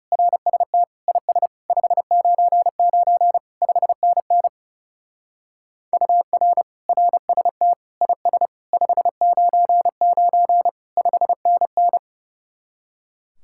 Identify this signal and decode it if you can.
Morse code, heard as a keyed tone, about 39 wpm, 700 Hz. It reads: RST IS 599 5NN UR RST IS 599 5NN